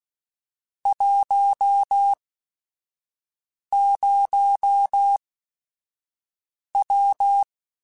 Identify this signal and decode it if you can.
Morse code, heard as a keyed tone, about 16 wpm, spaced as for 5 wpm, 800 Hz. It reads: 10W